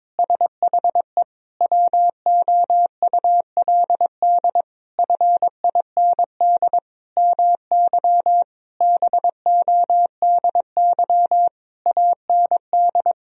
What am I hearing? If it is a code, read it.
Morse code, heard as a keyed tone, about 22 wpm, 700 Hz. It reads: SHE WOULD FIND MY BODY AND
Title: Sentences from Top 1000 Words - Review Part 2 - 22wpm